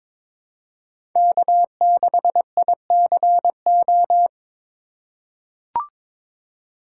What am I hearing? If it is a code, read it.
Morse code, heard as a keyed tone, about 22 wpm, 700 Hz. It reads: K6ICO
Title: USA Call Signs - 5 Characters - 2x Character Spacing - 22wpm